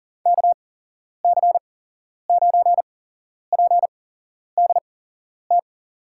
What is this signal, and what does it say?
Morse code, heard as a keyed tone, about 42 wpm, spaced as for 11 wpm, 700 Hz. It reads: KC9PDT
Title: USA Call Signs - 6 Characters - 8x Character Spacing - 40wpm